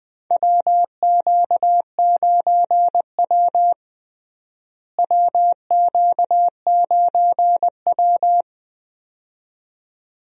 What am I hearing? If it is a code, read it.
Morse code, heard as a keyed tone, about 20 wpm, 700 Hz. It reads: WQ9W WQ9W
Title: USA Call Signs - 4 Characters - Repeated 2x - 20wpm